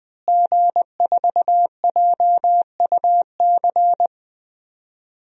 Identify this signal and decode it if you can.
Morse code, heard as a keyed tone, about 21 wpm, 700 Hz. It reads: G4JUC
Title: DXCC-100 Call Signs - 5 Characters - Repeated 3x - 20wpm